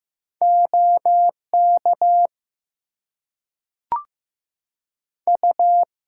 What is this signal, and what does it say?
Morse code, heard as a keyed tone, about 15 wpm, 700 Hz. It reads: OK U
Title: Two Letter Words - 15wpm